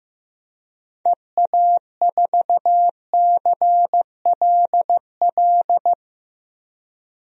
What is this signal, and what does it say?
Morse code, heard as a keyed tone, about 15 wpm, 700 Hz. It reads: EA4CLL
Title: DXCC-100 Call Signs - 6 Characters - 15wpm